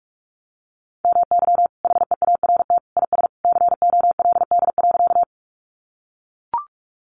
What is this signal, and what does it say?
Morse code, heard as a keyed tone, about 44 wpm, 700 Hz. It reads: MY HEART IS COLD.
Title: Sentences from Top 1200 Words 45wpm